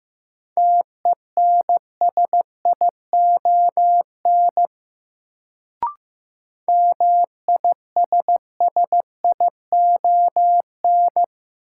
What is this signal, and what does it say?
Morse code, heard as a keyed tone, about 15 wpm, 700 Hz. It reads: TENSION MISSION
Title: Sentences from Top 1600 Words - New Word Drill Alt - 15wpm